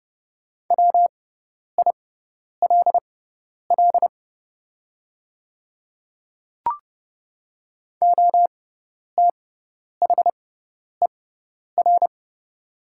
Morse code, heard as a keyed tone, about 30 wpm, spaced as for 10 wpm, 700 Hz. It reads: WILL OTHER
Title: 6x Character Spacing - Top 100 Words - 30wpm